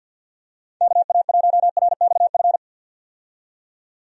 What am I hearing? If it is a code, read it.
Morse code, heard as a keyed tone, about 51 wpm, 700 Hz. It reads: KA1RXF
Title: USA Call Signs - 6 Characters - 50wpm